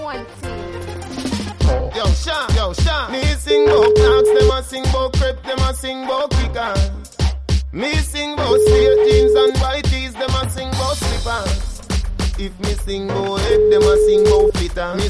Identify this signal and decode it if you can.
Ringback tone (Europe, 425 Hz), with speech.